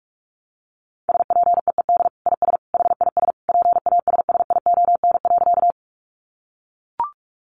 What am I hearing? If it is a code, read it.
Morse code, heard as a keyed tone, about 44 wpm, 700 Hz. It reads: SPEED IS HIS PASSION.
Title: Sentences from Top 1300 Words 45wpm